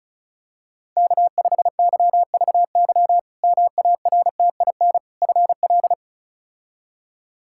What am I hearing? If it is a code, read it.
Morse code, heard as a keyed tone, about 34 wpm, 700 Hz. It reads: K5YVY MARTIN FL
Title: K1USN SST QSO Practice - 35wpm